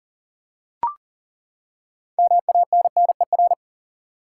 Morse code, heard as a keyed tone, about 40 wpm, 700 Hz. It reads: MANNER
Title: Six Letter Words - 40wpm